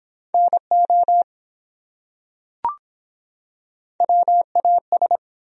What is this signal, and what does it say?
Morse code, heard as a keyed tone, about 24 wpm, 700 Hz. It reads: NO WAS